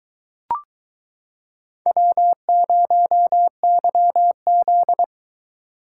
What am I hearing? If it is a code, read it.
Morse code, heard as a keyed tone, about 23 wpm, 700 Hz. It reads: W0YZ